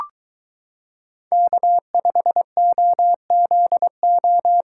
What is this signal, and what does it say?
Morse code, heard as a keyed tone, about 22 wpm, 700 Hz. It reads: K5OZO